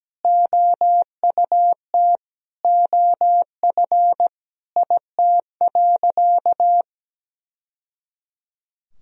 Morse code, heard as a keyed tone, about 17 wpm, 700 Hz. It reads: OUT OF IT.
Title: Sentences from Top 1000 Words - Review Part 1 - 17wpm